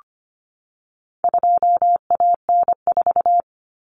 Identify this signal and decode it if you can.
Morse code, heard as a keyed tone, about 25 wpm, 700 Hz. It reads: 2AN4
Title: Common Cut Numbers - 4-Digit 25wpm